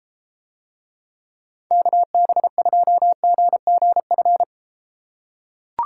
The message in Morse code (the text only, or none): KB2GGF